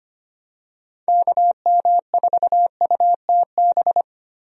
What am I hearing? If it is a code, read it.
Morse code, heard as a keyed tone, about 25 wpm, 700 Hz. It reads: KM4UTB
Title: USA Call Signs - 6 Characters - Repeated 2x - 25wpm